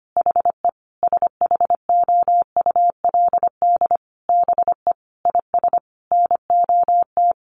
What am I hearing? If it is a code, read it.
Morse code, heard as a keyed tone, about 25 wpm, 700 Hz. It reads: HE SHOULD BE IS NOT
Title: Sentences from Top 1300 Words 25wpm